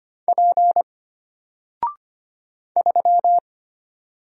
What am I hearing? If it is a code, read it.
Morse code, heard as a keyed tone, about 25 wpm, 700 Hz. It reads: P 3
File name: Single-Letter-Number-25wpm